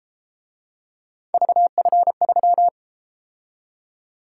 Morse code, heard as a keyed tone, about 33 wpm, 700 Hz. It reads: VF3